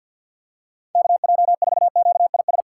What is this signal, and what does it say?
Morse code, heard as a keyed tone, about 52 wpm, 700 Hz. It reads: KJ4QIS
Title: USA Call Signs - 4x Character Spacing - 50wpm